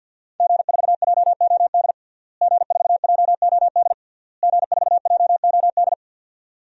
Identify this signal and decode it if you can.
Morse code, heard as a keyed tone, about 51 wpm, 700 Hz. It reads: G4JOD G4JOD G4JOD
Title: DXCC-100 Call Signs - 5 Characters - Repeated 3x - 50wpm